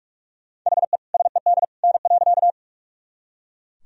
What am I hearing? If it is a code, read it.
Morse code, heard as a keyed tone, about 41 wpm, 700 Hz. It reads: HE SED N.
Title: Sentences from Top 100 Words and Abbreviations 45wpm